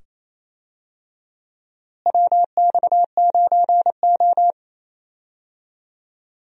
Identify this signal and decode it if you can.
Morse code, heard as a keyed tone, about 28 wpm, 700 Hz. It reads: WX9O
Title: USA Call Signs - 4 Characters - Repeated 2x - 28wpm